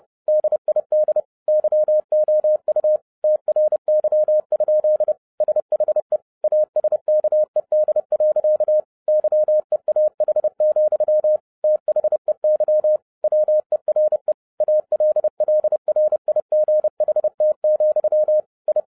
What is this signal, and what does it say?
Morse code, heard as a keyed tone, about 30 wpm, 610 Hz. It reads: DID YOU TRY? SHE ASKED. YEAH, THEY WERE ALLRIGHT, I